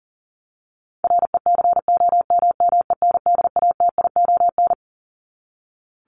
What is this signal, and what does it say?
Morse code, heard as a keyed tone, about 40 wpm, 700 Hz. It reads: RECOMMENDATION
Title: Top 1251-1500 Words 40wpm